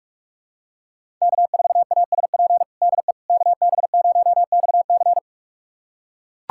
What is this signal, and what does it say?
Morse code, heard as a keyed tone, about 42 wpm, 700 Hz. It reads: K4ASP DE KB0XC
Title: USA Callsign DE Callsign - 5 Characters - 45wpm